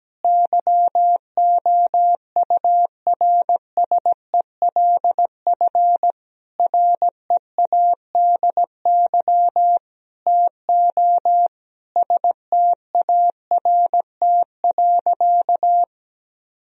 Morse code, heard as a keyed tone, about 17 wpm, 700 Hz. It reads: YOURSELF READY TO START.
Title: Sentences from Top 1000 Words - Review Part 1 - 17wpm